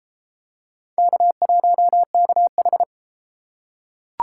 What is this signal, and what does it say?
Morse code, heard as a keyed tone, about 31 wpm, 700 Hz. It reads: K1KH